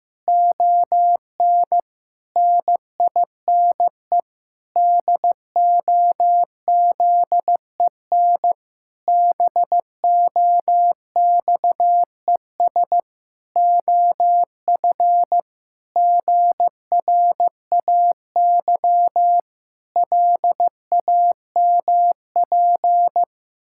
Morse code, heard as a keyed tone, about 15 wpm, 700 Hz. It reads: ON NINE DOZEN BOXES OF GRAY LAMP
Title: Pangram Challenge v2 - 15wpm